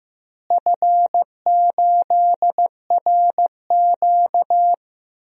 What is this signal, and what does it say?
Morse code, heard as a keyed tone, about 15 wpm, 700 Hz. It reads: F8RQ
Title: DXCC-100 Call Signs - 4 Characters - Repeated 3x - 15wpm